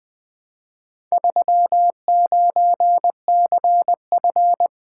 Morse code, heard as a keyed tone, about 20 wpm, 700 Hz. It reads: 39CF
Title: Four Letter-Number Combinations - 20wpm